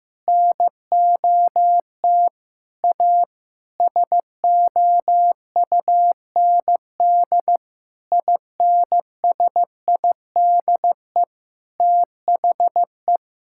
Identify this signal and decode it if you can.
Morse code, heard as a keyed tone, about 15 wpm, 700 Hz. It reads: NOT A SOUND INSIDE THE
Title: Sentences from Top 1000 Words - Review Part 6 - 15wpm